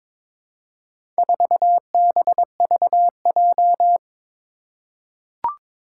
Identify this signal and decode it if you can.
Morse code, heard as a keyed tone, about 22 wpm, 700 Hz. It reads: 4BVJ
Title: Four Challenging Letter-Number Combinations 22wpm